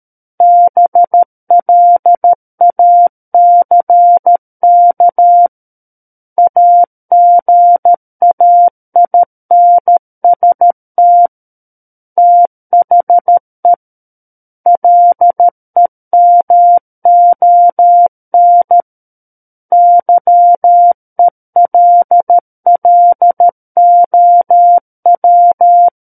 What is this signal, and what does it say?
Morse code, heard as a keyed tone, about 13 wpm, 700 Hz. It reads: BLACK AGAINST THE LEMON YELLOW
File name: Book One / 04-Chapter 4 - 13 WPM